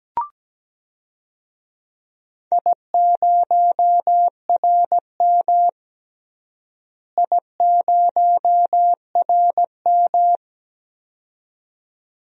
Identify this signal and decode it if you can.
Morse code, heard as a keyed tone, about 17 wpm, 700 Hz. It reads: I0RM I0RM
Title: DXCC-100 Call Signs - 4 Characters - Repeated 2x - 17wpm